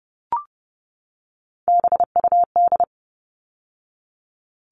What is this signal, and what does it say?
Morse code, heard as a keyed tone, about 30 wpm, 700 Hz. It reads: BUD
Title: All Cut Numbers - 3-Digit 30wpm